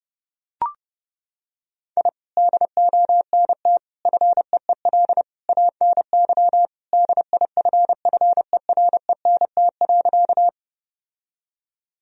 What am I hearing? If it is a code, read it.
Morse code, heard as a keyed tone, about 30 wpm, 700 Hz. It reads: I DONT FEEL ANY DIFFERENT.
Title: Sentences from Top 1000 Words - Review Part 3 - 30wpm